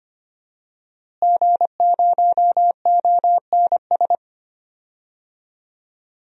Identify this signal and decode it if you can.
Morse code, heard as a keyed tone, about 25 wpm, 700 Hz. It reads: G0ONS
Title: DXCC-100 Call Signs - 5 Characters - Repeated 3x - 25wpm